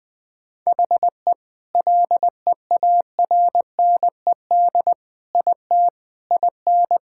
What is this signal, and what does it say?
Morse code, heard as a keyed tone, about 20 wpm, 700 Hz. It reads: HE LEARNED IT IN
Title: Sentences with 7-Word Phrases - 20wpm